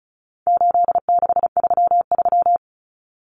Morse code, heard as a keyed tone, about 34 wpm, 700 Hz. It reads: 8633
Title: Numbers - 4-Digit 35wpm